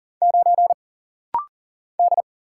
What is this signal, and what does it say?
Morse code, heard as a keyed tone, about 40 wpm, 700 Hz. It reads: 9 D
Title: Single Letter-Number - Rapid-Fire 40wpm